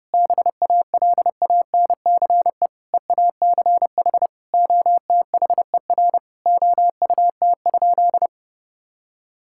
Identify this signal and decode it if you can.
Morse code, heard as a keyed tone, about 32 wpm, 700 Hz. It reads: BALANCE EACH OTHER OUT?
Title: Sentences from Top 1000 Words - Review Part 6 - 30wpm